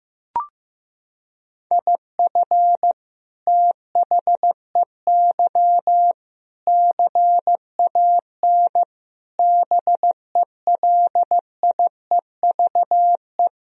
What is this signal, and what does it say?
Morse code, heard as a keyed tone, about 16 wpm, 700 Hz. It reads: IF THEY CAN BELIEVE
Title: Sentences from Top 1000 Words - Review Part 2 - 15wpm